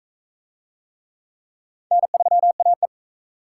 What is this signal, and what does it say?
Morse code, heard as a keyed tone, about 39 wpm, 700 Hz. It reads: N3AE